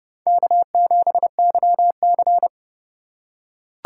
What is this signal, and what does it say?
Morse code, heard as a keyed tone, about 30 wpm, 700 Hz. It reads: K7YC
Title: USA Call Signs - 4 Characters - 2x Character Spacing - 30wpm